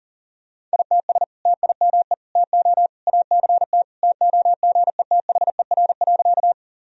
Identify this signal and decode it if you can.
Morse code, heard as a keyed tone, about 40 wpm, 700 Hz. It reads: ITS TIME TO ACT TOGETHER.
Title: Sentences from Top 1000 Words - Review Part 1 - 40wpm